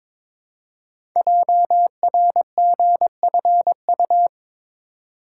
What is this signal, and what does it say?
Morse code, heard as a keyed tone, about 22 wpm, 690 Hz. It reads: JRGFU
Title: Code Groups - 5 Letters - 22wpm